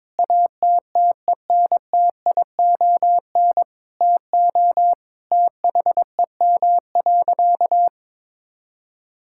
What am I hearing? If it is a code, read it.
Morse code, heard as a keyed tone, about 22 wpm, 700 Hz. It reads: ATTENTION TO THEM.
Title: Sentences from Top 1000 Words - Review Part 6 - 22wpm